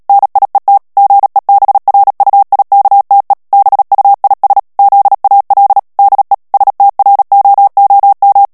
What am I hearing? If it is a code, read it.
Morse code, heard as a keyed tone, about 37 wpm, 800 Hz. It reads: NIET GEBRUIKTE BUIS ZAL DE STROOM
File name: morse mp3 / HOLLANDAIS texte clair 2 vitesse 35 mots minute